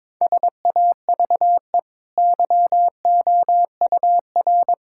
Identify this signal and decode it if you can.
Morse code, heard as a keyed tone, about 21 wpm, 690 Hz. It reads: SAVE YOUR
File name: practice / 22wpm / Easy Sentences with Hints - 22wpm